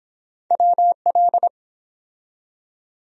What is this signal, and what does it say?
Morse code, heard as a keyed tone, about 24 wpm, 700 Hz. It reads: WL